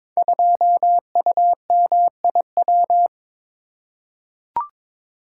Morse code, heard as a keyed tone, about 23 wpm, 700 Hz. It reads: 2UMIW
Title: Code Groups - 5 Letter:Numbers - 22wpm